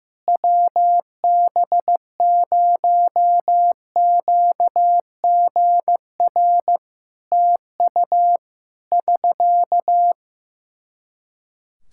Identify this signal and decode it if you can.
Morse code, heard as a keyed tone, about 16 wpm, 700 Hz. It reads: WB0QGR TU <SK>
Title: Practice QSO - Exchange 4 - Ending - 15wpm